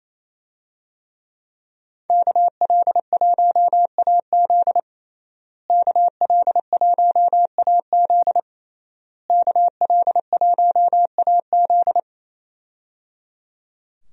Morse code, heard as a keyed tone, about 28 wpm, 700 Hz. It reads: KL1AZ KL1AZ KL1AZ
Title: USA Call Signs - 5 Characters - Repeated 3x - 28wpm